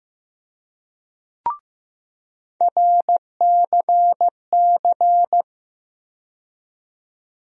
Morse code, heard as a keyed tone, about 15 wpm, 700 Hz. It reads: RCC
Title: USA Call Sign Postfixes - 3 Letters 15wpm